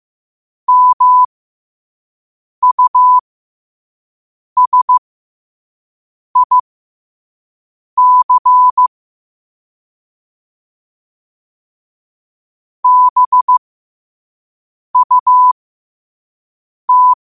Morse code, heard as a keyed tone, about 15 wpm, spaced as for 5 wpm, 1000 Hz. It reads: MUSIC BUT